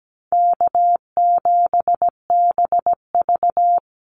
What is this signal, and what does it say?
Morse code, heard as a keyed tone, about 17 wpm, 700 Hz. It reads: K7BV
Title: Call Signs Repeated 3x - 17wpm